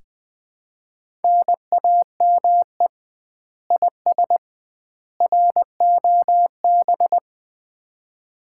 Morse code, heard as a keyed tone, about 21 wpm, 700 Hz. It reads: NAME IS ROB
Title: Common QSO Phrases - 2x Word-Spacing - 20wpm